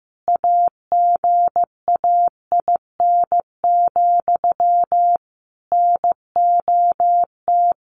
Morse code, heard as a keyed tone, about 15 wpm, 700 Hz. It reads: AGAIN, NOT
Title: A Sound Of Thunder 15wpm